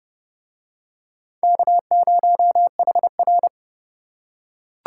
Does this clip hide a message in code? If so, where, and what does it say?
Yes: Morse code from 1.4 to 3.5 s, heard as a keyed tone, about 32 wpm, 700 Hz. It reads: K0HR